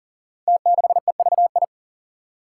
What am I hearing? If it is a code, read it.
Morse code, heard as a keyed tone, about 43 wpm, 700 Hz. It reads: T6EVI
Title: Five Challenging Letter-Number Combinations 40wpm